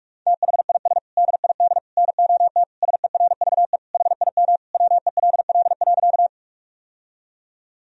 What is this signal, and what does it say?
Morse code, heard as a keyed tone, about 41 wpm, 700 Hz. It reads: THIS DID NOT SERVE HIM WELL.
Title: Sentences from Top 1000 Words - Review Part 2 - 45wpm